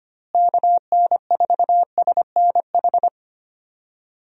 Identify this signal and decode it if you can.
Morse code, heard as a keyed tone, about 25 wpm, 700 Hz. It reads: KN4SNH
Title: USA Call Signs - 6 Characters - Repeated 2x - 25wpm